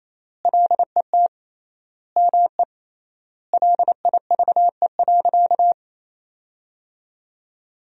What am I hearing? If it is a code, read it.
Morse code, heard as a keyed tone, about 28 wpm, 700 Hz. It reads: LET ME LIVE.
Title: 3x Word-Spacing - Sentences from Top 300 Words - 28wpm